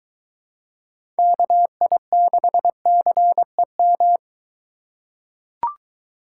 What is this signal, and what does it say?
Morse code, heard as a keyed tone, about 23 wpm, 700 Hz. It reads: KI6CEM